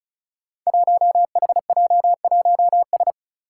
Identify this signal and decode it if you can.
Morse code, heard as a keyed tone, about 36 wpm, 700 Hz. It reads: 1HJ1S
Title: Five Challenging Letter-Number Combinations 35wpm